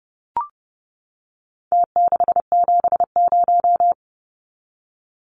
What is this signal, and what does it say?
Morse code, heard as a keyed tone, about 30 wpm, 700 Hz. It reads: T670
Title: Numbers - Contest Style - 4-Digit 30wpm